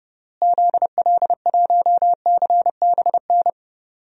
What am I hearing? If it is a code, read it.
Morse code, heard as a keyed tone, about 32 wpm, 700 Hz. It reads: ZL1CBN
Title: DXCC-100 Call Signs - 6 Characters - Repeated 3x - 30wpm